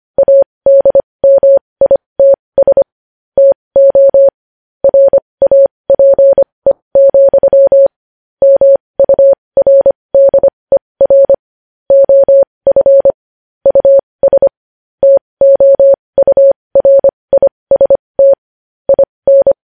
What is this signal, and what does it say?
Morse code, heard as a keyed tone, about 25 wpm, spaced as for 20 wpm, 550 Hz. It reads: ADMITS TO RAPE, MURDER OF US TOURIST IN
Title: News Headlines at 20